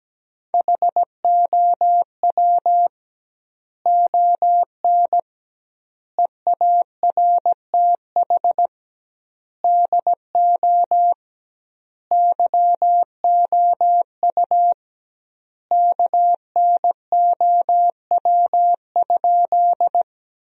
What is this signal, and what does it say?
Morse code, heard as a keyed tone, about 17 wpm, 700 Hz. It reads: HOW ON EARTH DO YOU KNOW?
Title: 2x Word-Spacing - Sentences from Top 200 Words - 17wpm